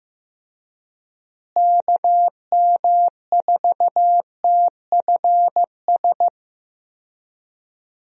Morse code, heard as a keyed tone, about 15 wpm, 700 Hz. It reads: KM4TFS
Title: USA Call Signs - 6 Characters - Repeated 3x - 15wpm